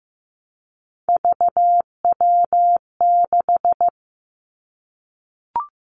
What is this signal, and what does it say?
Morse code, heard as a keyed tone, about 15 wpm, 700 Hz. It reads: VW6
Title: Three Challenging Letter-Number Combinations 15wpm